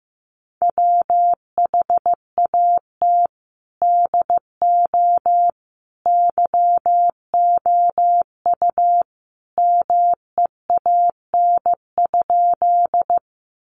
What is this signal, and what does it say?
Morse code, heard as a keyed tone, about 15 wpm, 700 Hz. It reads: WHAT DO YOU MEAN?
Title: Sentences from Top 200 Words 15wpm